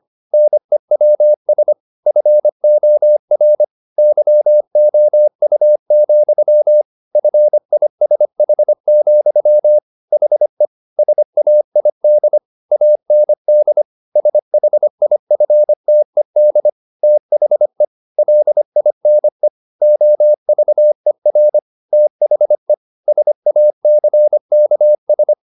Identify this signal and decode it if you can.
Morse code, heard as a keyed tone, about 25 wpm, 600 Hz. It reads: NEWS FOR YOU, FISH, HE SAID AND SHIFTED THE LINE OVER THE SACKS
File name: the old man and the sea part 1 25wpm